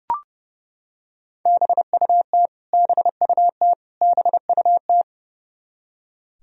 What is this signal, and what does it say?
Morse code, heard as a keyed tone, about 30 wpm, 700 Hz. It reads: BUT BUT BUT
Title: Top 100 Words Repeated 30wpm